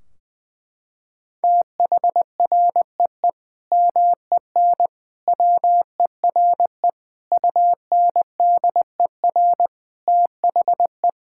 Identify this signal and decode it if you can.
Morse code, heard as a keyed tone, about 21 wpm, 700 Hz. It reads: THREE MEN WERE UNDER THE
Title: Sentences from Top 1600 Words 20wpm